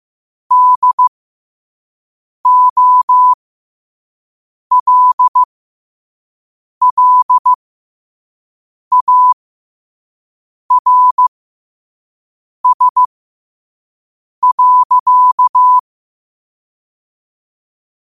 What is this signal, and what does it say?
Morse code, heard as a keyed tone, about 15 wpm, spaced as for 5 wpm, 1000 Hz. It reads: DOLLARS.
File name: qotdfiles / 5wpm